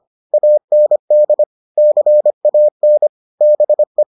Morse code, heard as a keyed tone, about 25 wpm, 600 Hz. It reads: AND CAN BE